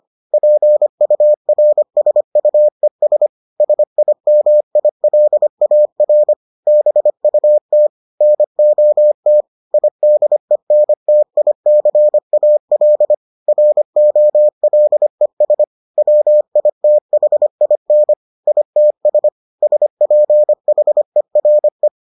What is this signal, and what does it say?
Morse code, heard as a keyed tone, about 25 wpm, 600 Hz. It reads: PURSUES SIMILAR BUT NOT IDENTICAL ROLES WITHIN ITS SPHERE